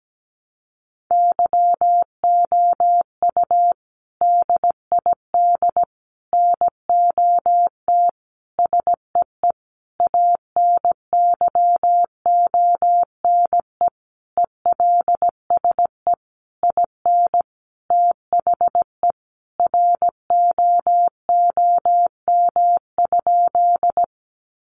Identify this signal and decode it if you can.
Morse code, heard as a keyed tone, about 17 wpm, 700 Hz. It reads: YOU DID NOT SEE ANYONE ELSE IN THE ROOM?
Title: Sentences from Top 900 Words 17wpm